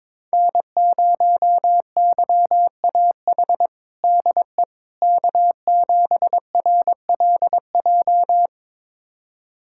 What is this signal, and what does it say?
Morse code, heard as a keyed tone, about 22 wpm, 700 Hz. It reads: N0YAH DE K7RLJ
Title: USA Callsign DE Callsign - 5 Characters - 22wpm